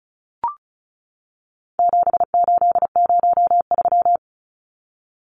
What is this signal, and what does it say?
Morse code, heard as a keyed tone, about 34 wpm, 700 Hz. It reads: 7803
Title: Numbers - 4-Digit 35wpm